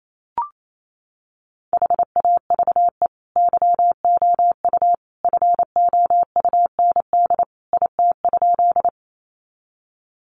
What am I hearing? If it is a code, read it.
Morse code, heard as a keyed tone, about 28 wpm, 700 Hz. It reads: HAVE YOU FOUND IT?
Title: Sentences from Top 300 Words 28wpm